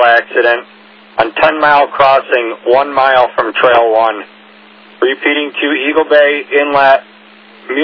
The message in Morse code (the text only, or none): none